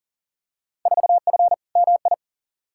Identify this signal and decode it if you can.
Morse code, heard as a keyed tone, about 40 wpm, 700 Hz. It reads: 4F MI
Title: Field Day Exchanges Encore - 40wpm